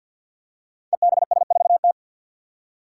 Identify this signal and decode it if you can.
Morse code, heard as a keyed tone, about 49 wpm, 700 Hz. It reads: EBS4T
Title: Five Challenging Letter-Number Combinations 50wpm